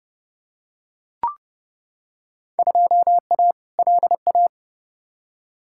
Morse code, heard as a keyed tone, about 31 wpm, 700 Hz. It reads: 2A LA
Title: Field Day Exchanges - 30wpm